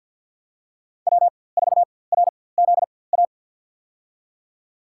Morse code, heard as a keyed tone, about 51 wpm, spaced as for 23 wpm, 700 Hz. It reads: W4RZA
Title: USA Call Signs - 5 Characters - 4x Character Spacing - 50wpm